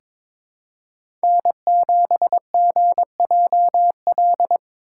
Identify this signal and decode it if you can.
Morse code, heard as a keyed tone, about 22 wpm, 700 Hz. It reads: N7GJL
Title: USA Call Signs - 3x Character Spacing - 22wpm